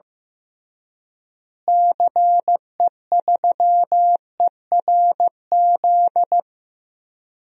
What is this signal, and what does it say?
Morse code, heard as a keyed tone, about 15 wpm, 700 Hz. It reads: CE3ERZ
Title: DXCC-100 Call Signs - 6 Characters - Repeated 3x - 15wpm